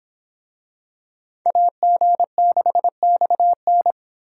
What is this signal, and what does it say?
Morse code, heard as a keyed tone, about 26 wpm, 700 Hz. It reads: AG6XN